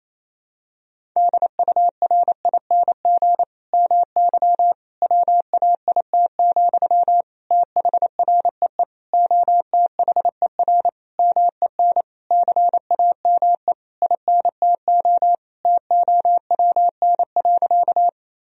Morse code, heard as a keyed tone, about 27 wpm, 700 Hz. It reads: DURING MY WAIT, THREE OTHER MEN CAME INTO TOWN.
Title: Sentences from Top 1000 Words - Review Part 2 - 28wpm